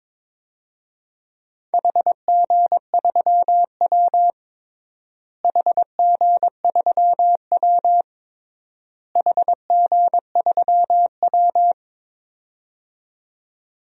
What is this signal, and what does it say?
Morse code, heard as a keyed tone, about 22 wpm, 700 Hz. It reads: HG3W HG3W HG3W
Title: DXCC-100 Call Signs - 4 Characters - Repeated 3x - 22wpm